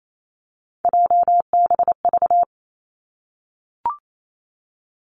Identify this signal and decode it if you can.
Morse code, heard as a keyed tone, about 28 wpm, 700 Hz. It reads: JBV
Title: Three Challenging Letter-Number Combinations 28wpm